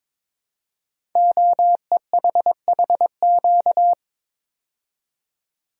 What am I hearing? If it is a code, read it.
Morse code, heard as a keyed tone, about 23 wpm, 700 Hz. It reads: OEHHQ
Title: Code Groups - 5 Letters - 22wpm